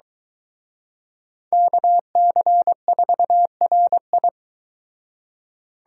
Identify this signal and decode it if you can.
Morse code, heard as a keyed tone, about 22 wpm, 700 Hz. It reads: KC4RI